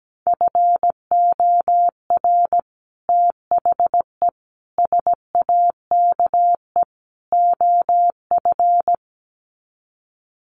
Morse code, heard as a keyed tone, about 17 wpm, 700 Hz. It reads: FOR THE SAKE OF
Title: Sets of 4 Words 17wpm